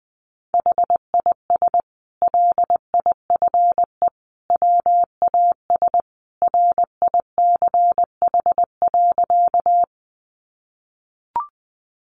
Morse code, heard as a keyed tone, about 20 wpm, 700 Hz. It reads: HIS LIFE WAS RICH.
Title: Sentences from Top 1100 Words 20wpm